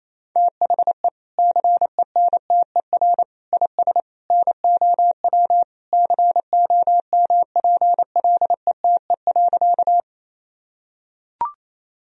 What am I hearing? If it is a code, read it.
Morse code, heard as a keyed tone, about 28 wpm, 700 Hz. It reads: THE CENTER IS NOW COMPLETE.
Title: Sentences from Top 1000 Words - Review Part 2 - 28wpm